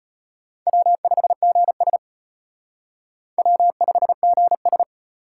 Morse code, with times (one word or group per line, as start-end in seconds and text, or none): 0.7-2.0 W5GS
3.4-4.8 W5GS